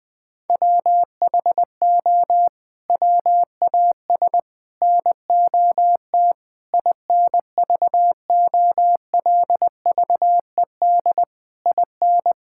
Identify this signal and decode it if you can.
Morse code, heard as a keyed tone, about 21 wpm, 700 Hz. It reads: WHO WAS NOT INVOLVED IN
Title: Sets of 6 Words - 20wpm